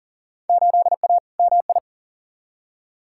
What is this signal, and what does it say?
Morse code, heard as a keyed tone, about 40 wpm, 700 Hz. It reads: 8A MI
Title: Field Day Exchanges Encore - 40wpm